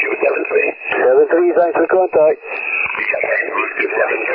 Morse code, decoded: none